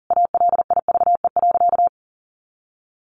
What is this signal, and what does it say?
Morse code, heard as a keyed tone, about 40 wpm, 700 Hz. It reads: ALIVE.
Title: A Sound Of Thunder 40wpm